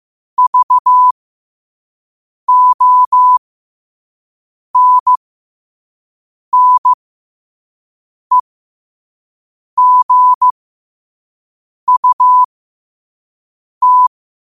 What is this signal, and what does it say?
Morse code, heard as a keyed tone, about 15 wpm, spaced as for 5 wpm, 1000 Hz. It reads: VONNEGUT
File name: qotdfiles / 5wpm